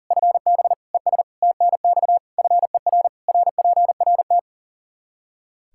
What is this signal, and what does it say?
Morse code, heard as a keyed tone, about 40 wpm, 700 Hz. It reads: FB ES TNX FER RPRT
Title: Practice QSO - Exchange 2 - Info - 40wpm